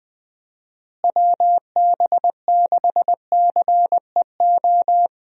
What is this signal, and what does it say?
Morse code, heard as a keyed tone, about 20 wpm, 700 Hz. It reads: WB6CEO
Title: USA Call Signs - 6 Characters - Repeated 3x - 20wpm